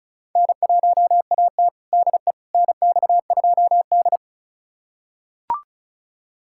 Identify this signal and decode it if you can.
Morse code, heard as a keyed tone, about 36 wpm, 700 Hz. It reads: N1AT DE NX2D
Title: USA Callsign DE Callsign - 4 Characters - 35wpm